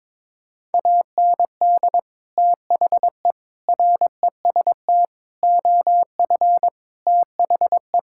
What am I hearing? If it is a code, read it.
Morse code, heard as a keyed tone, about 22 wpm, 700 Hz. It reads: AND THE REST OF THE
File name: practice / 22wpm / Sentences with 4-Word Phrases - 22wpm